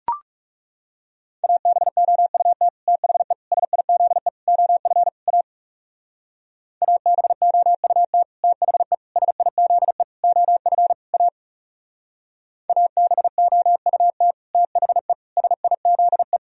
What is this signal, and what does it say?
Morse code, heard as a keyed tone, about 35 wpm, 700 Hz. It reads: ABOUT THE SIZE OF A ABOUT THE SIZE OF A ABOUT THE SIZE
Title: Speed-Racing - Sets of 5 Words 30wpm